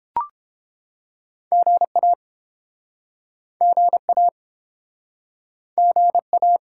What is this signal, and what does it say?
Morse code, heard as a keyed tone, about 32 wpm, 700 Hz. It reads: GA GA GA